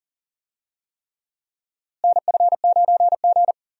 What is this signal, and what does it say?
Morse code, heard as a keyed tone, about 40 wpm, 700 Hz. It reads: NF9G
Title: USA Call Signs - 4 Characters - Repeated 3x - 40wpm